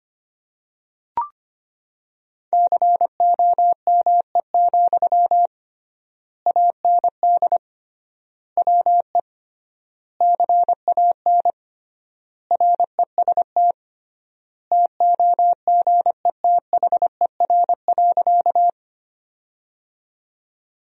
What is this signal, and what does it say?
Morse code, heard as a keyed tone, about 25 wpm, 700 Hz. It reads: COME, AND WE CAN REST TOGETHER.
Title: 3x Word-Spacing - Sentences from Top 500 Words - 25wpm